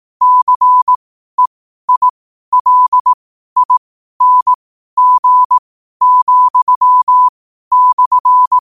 Morse code, heard as a keyed tone, about 18 wpm, spaced as for 12 wpm, 1000 Hz. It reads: CEILING,/
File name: qotdfiles / 12wpm